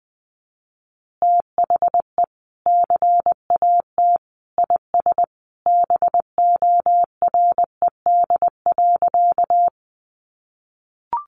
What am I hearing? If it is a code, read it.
Morse code, heard as a keyed tone, about 20 wpm, 700 Hz. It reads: THE CAT IS BORED.
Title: Easy Phrases 20wpm